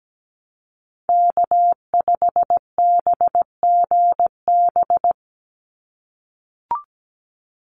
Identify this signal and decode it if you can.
Morse code, heard as a keyed tone, about 17 wpm, 700 Hz. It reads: K5BGB
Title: Call Signs - 17wpm